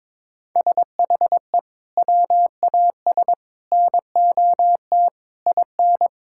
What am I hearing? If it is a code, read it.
Morse code, heard as a keyed tone, about 22 wpm, 700 Hz. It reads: SHE WAS NOT IN
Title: Sentences from Top 1600 Words 22wpm